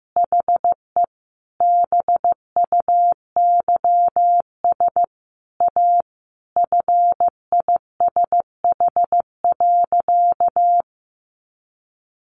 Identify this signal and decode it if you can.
Morse code, heard as a keyed tone, about 15 wpm, 700 Hz. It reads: HE BUYS A FISH.
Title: Easy Phrases 15wpm